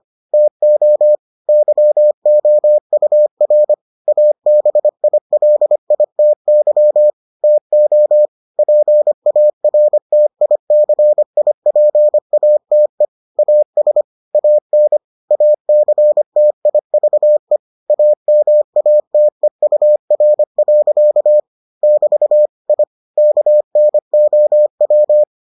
Morse code, heard as a keyed tone, about 25 wpm, 600 Hz. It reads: TO YOUR ABILITY TO PARTICIPATE AS AN ACTIVE AMATEUR. <BT> I KNOW